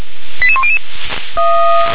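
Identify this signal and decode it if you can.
Five-tone selective call (ZVEI), heard as quick runs of five tones, tones of about 70 ms. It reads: 88100
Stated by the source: ZVEI five-tone selective call received via SDR.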